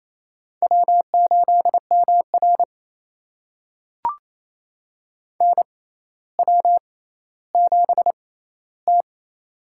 Morse code, heard as a keyed tone, about 28 wpm, spaced as for 14 wpm, 700 Hz. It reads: W8MR NW7T
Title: USA Call Signs - 4 Characters - 6x Character Spacing - 28wpm